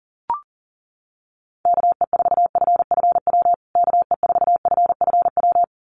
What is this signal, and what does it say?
Morse code, heard as a keyed tone, about 40 wpm, 700 Hz. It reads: KE4FFW KE4FFW
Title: USA Call Signs Repeated 2x 40wpm